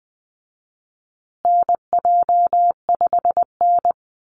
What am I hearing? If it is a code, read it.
Morse code, heard as a keyed tone, about 20 wpm, 700 Hz. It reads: NJ5N
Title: Call Signs Repeated 3x - 20wpm